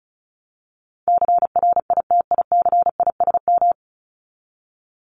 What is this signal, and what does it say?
Morse code, heard as a keyed tone, about 34 wpm, 700 Hz. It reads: CRITICISM
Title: Top 1001-1250 Words 35wpm